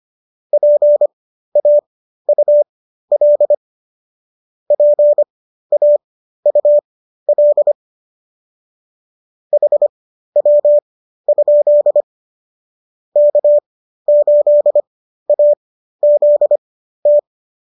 Morse code, heard as a keyed tone, about 25 wpm, spaced as for 13 wpm, 590 Hz. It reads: PAUL PAUL HW? K8AZT